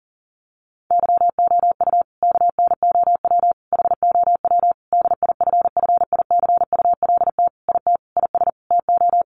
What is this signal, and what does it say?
Morse code, heard as a keyed tone, about 40 wpm, 700 Hz. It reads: YOU KNOW HOW DIFFICULT IT IS TO
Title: Sentences from Top 1100 Words 40wpm